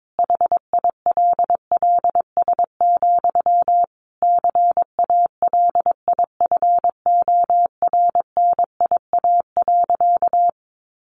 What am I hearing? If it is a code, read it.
Morse code, heard as a keyed tone, about 22 wpm, 700 Hz. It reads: HILLS, CALIFORNIA.
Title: Easy Phrases 22wpm